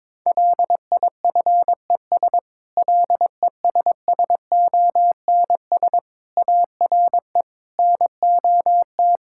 Morse code, heard as a keyed tone, about 21 wpm, 700 Hz. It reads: LIFES LESSONS ARE NOT
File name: Difficult Sentences with Hints - 22wpm